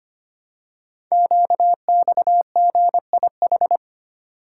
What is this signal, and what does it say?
Morse code, heard as a keyed tone, about 25 wpm, 700 Hz. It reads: QXGIH
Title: Code Groups - 5 Letters - 25wpm